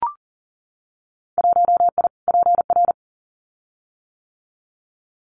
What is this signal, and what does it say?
Morse code, heard as a keyed tone, about 40 wpm, 700 Hz. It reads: 1I PR
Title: Winter Field Day Exchanges Encore 40wpm